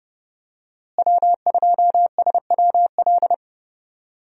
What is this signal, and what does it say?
Morse code, heard as a keyed tone, about 30 wpm, 700 Hz. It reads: W2SWL
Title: USA Call Signs - 5 Characters - Repeated 2x - 30wpm